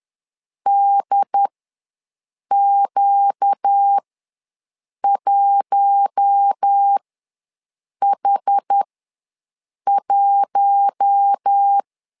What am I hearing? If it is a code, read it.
Morse code, heard as a keyed tone, about 10 wpm, spaced as for 6 wpm, 800 Hz. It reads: DQ1H1
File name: LEZIONI / 4 Q1S ws 10wpm 2208